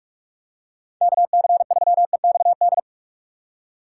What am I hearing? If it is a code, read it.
Morse code, heard as a keyed tone, about 46 wpm, 700 Hz. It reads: KC3EXD
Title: USA Call Signs - 3x Character Spacing - 45wpm